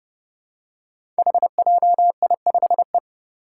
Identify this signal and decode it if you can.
Morse code, heard as a keyed tone, about 28 wpm, 700 Hz. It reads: HJI5E
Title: Five Challenging Letter-Number Combinations 30wpm